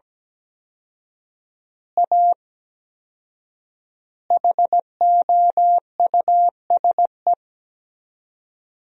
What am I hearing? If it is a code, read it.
Morse code, heard as a keyed tone, about 17 wpm, 700 Hz. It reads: A HOUSE
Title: 4x Word-Spacing - Sentences from Top 200 Words - 17wpm